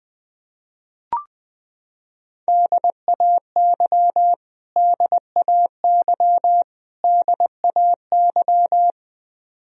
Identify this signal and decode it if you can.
Morse code, heard as a keyed tone, about 20 wpm, 700 Hz. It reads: DAY DAY DAY
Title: Top 100 Words Repeated 20wpm